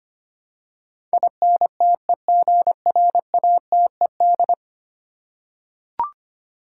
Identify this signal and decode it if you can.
Morse code, heard as a keyed tone, about 25 wpm, 700 Hz. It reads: INTEGRATED
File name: 10 Letter Words - First 3 Letters Given - 25wpm